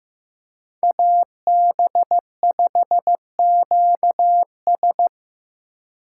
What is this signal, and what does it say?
Morse code, heard as a keyed tone, about 15 wpm, 700 Hz. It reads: AB5QS
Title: Speed Racing - USA 2x2 Call Signs 15wpm